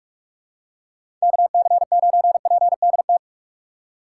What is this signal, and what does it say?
Morse code, heard as a keyed tone, about 46 wpm, 700 Hz. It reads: KC9PDT
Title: USA Call Signs - 3x Character Spacing - 45wpm